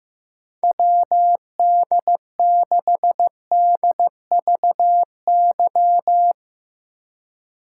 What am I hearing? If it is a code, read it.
Morse code, heard as a keyed tone, about 15 wpm, 700 Hz. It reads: WD6DVY
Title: Speed Racing - USA 2x3 Call Signs 15wpm